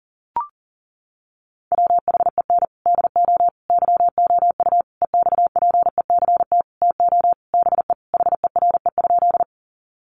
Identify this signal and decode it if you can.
Morse code, heard as a keyed tone, about 40 wpm, 700 Hz. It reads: WHEN DO YOU EXPECT TO BE HERE?
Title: Sentences from Top 1100 Words 40wpm